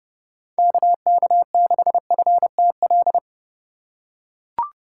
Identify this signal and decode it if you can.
Morse code, heard as a keyed tone, about 30 wpm, 700 Hz. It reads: KK6FTL